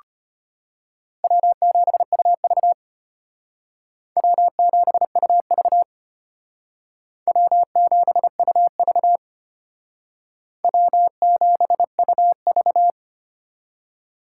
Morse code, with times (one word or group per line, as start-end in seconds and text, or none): 1.2-2.7 W7UV
4.2-5.8 W7UV
7.3-9.2 W7UV
10.6-12.9 W7UV